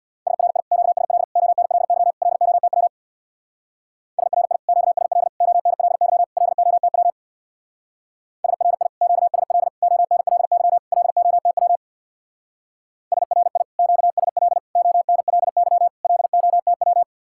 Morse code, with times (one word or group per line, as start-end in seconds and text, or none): 0.3-0.6 SRI
0.7-1.2 QSL
1.4-2.1 ONLY
2.2-2.9 LOTW
4.2-4.6 SRI
4.7-5.3 QSL
5.4-6.2 ONLY
6.4-7.1 LOTW
8.4-8.9 SRI
9.0-9.7 QSL
9.8-10.8 ONLY
10.9-11.8 LOTW
13.1-13.6 SRI
13.8-14.6 QSL
14.8-15.9 ONLY
16.0-17.0 LOTW